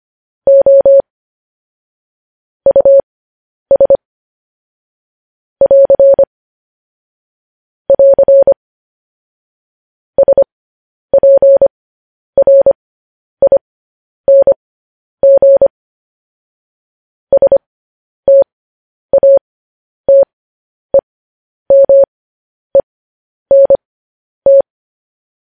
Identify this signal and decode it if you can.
Morse code, heard as a keyed tone, about 25 wpm, spaced as for 10 wpm, 550 Hz. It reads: O US <AR> <AR> SPRING STATEMENT
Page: news-headlines-in-morse-code-at-10-wpm